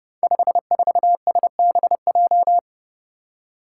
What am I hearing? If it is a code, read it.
Morse code, heard as a keyed tone, about 28 wpm, 700 Hz. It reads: 54SBJ
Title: Five Challenging Letter-Number Combinations 30wpm